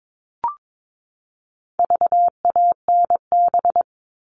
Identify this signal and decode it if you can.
Morse code, heard as a keyed tone, about 22 wpm, 700 Hz. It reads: VANB
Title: All Cut Numbers - 4-Digit 22wpm